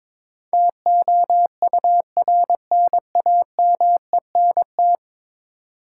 Morse code, heard as a keyed tone, about 22 wpm, 700 Hz. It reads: TOURNAMENT
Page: morse-code-ninja-practice-22wpm